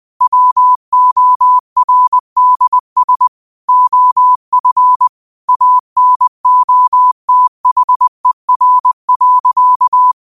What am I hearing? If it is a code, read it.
Morse code, heard as a keyed tone, about 20 wpm, 1000 Hz. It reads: WORDS OF ANOTHER.
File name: 20wpm